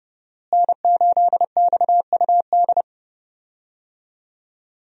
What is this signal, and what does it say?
Morse code, heard as a keyed tone, about 30 wpm, 700 Hz. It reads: N8XUD
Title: USA Call Signs - 5 Characters - Repeated 2x - 30wpm